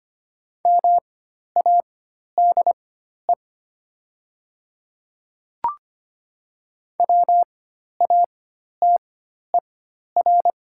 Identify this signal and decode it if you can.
Morse code, heard as a keyed tone, about 25 wpm, spaced as for 12 wpm, 700 Hz. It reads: MADE WATER
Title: 4x Character Spacing - Top 200 Words - 25wpm